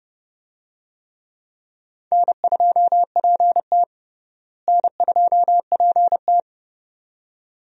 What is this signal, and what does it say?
Morse code, heard as a keyed tone, about 30 wpm, 700 Hz. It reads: N2PT N2PT
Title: USA Call Signs - 4 Characters - Repeated 2x - 30wpm